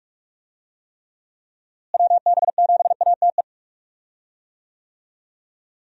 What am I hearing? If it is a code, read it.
Morse code, heard as a keyed tone, about 47 wpm, 700 Hz. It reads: WB7ATE
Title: USA Call Signs - 6 Characters - Repeated 2x - 45wpm